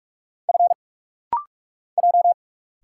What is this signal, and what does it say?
Morse code, heard as a keyed tone, about 46 wpm, 690 Hz. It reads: F J
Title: Single Letters - Rapid-Fire 45wpm